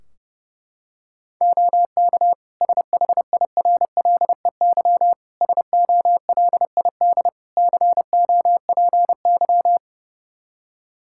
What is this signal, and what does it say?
Morse code, heard as a keyed tone, about 30 wpm, 700 Hz. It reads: OK SHIRLEY SOLID COPY
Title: Practice QSO - Exchange 3 - More Info - 30wpm